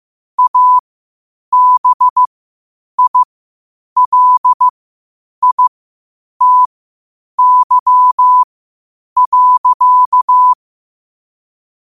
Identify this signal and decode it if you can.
Morse code, heard as a keyed tone, about 15 wpm, spaced as for 8 wpm, 1000 Hz. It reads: ABILITY.
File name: qotdfiles / 8wpm